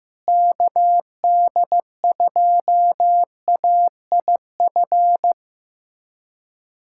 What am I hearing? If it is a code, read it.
Morse code, heard as a keyed tone, about 15 wpm, 700 Hz. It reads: KD2AIF
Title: USA Callsign DE Callsign - 6 Characters - 15wpm